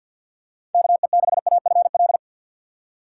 Morse code, heard as a keyed tone, about 53 wpm, 700 Hz. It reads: KE6AFL
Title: USA Call Signs - 6 Characters - 6x Character Spacing - 50wpm